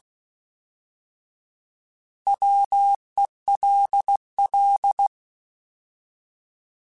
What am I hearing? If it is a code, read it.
Morse code, heard as a keyed tone, about 16 wpm, 800 Hz. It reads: WELL
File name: Alphabetical / Alphabetical mp3s / Ws @15